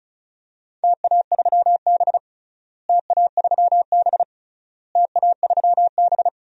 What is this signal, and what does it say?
Morse code, heard as a keyed tone, about 36 wpm, 700 Hz. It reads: TA3B TA3B TA3B
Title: DXCC-100 Call Signs - 4 Characters - Repeated 3x - 35wpm